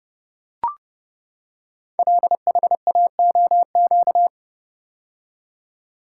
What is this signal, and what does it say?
Morse code, heard as a keyed tone, about 30 wpm, 700 Hz. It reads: LHAOQ
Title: Code Groups - 5 Letters - 30wpm